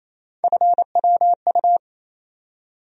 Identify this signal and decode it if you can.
Morse code, heard as a keyed tone, about 28 wpm, 700 Hz. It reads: FWU
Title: USA Call Sign Postfixes - 3 Letters 28wpm